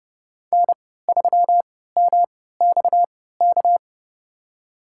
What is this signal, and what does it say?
Morse code, heard as a keyed tone, about 30 wpm, spaced as for 17 wpm, 700 Hz. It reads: N3MXK
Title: USA Call Signs - 5 Characters - 3x Character Spacing - 30wpm